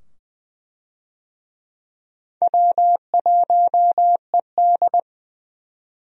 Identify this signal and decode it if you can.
Morse code, heard as a keyed tone, about 21 wpm, 700 Hz. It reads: W1ED
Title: USA Call Signs - 4 Characters - 20wpm